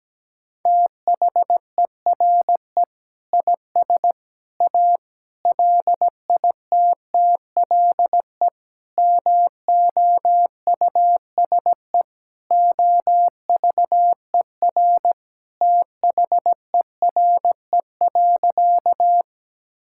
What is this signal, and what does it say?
Morse code, heard as a keyed tone, about 17 wpm, 700 Hz. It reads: THERE IS A LITTLE MOUSE OVER THERE.
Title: Sentences from Top 1000 Words - Review Part 5 - 17wpm